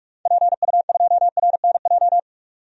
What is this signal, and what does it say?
Morse code, heard as a keyed tone, about 43 wpm, 700 Hz. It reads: PU2RNJ
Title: DXCC-100 Call Signs - 6 Characters - 45wpm